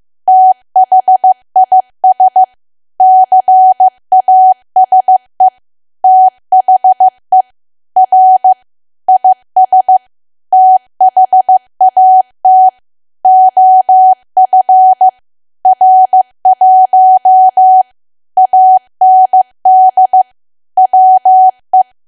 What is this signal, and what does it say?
Morse code, heard as a keyed tone, about 15 wpm, 750 Hz. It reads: THIS CASE THE R IS THAT OF R1 AND WE
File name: cwa / W1AW-Level2 / 15WPM